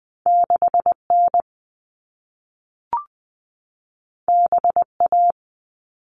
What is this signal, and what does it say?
Morse code, heard as a keyed tone, about 20 wpm, 700 Hz. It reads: 6N BA
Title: All Cut Numbers - 2-Digit 20wpm